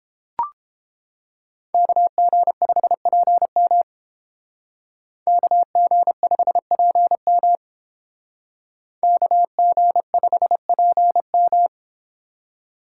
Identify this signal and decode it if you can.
Morse code, heard as a keyed tone, about 31 wpm, 700 Hz. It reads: KG5PM KG5PM KG5PM